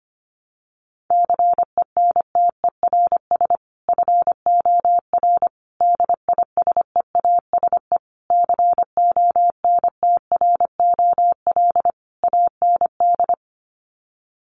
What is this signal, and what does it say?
Morse code, heard as a keyed tone, about 25 wpm, 700 Hz. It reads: CENTERS FOR DISEASE CONTROL AND
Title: Sets of 5 Words Encore2 25wpm